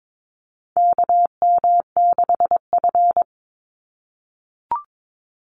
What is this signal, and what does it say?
Morse code, heard as a keyed tone, about 22 wpm, 700 Hz. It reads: KM6F
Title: USA Call Signs 22wpm